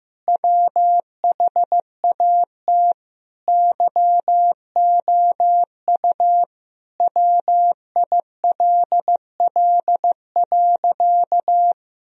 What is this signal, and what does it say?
Morse code, heard as a keyed tone, about 16 wpm, 700 Hz. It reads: WHAT YOU WILL.
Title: Sentences from Top 1000 Words - Review Part 2 - 15wpm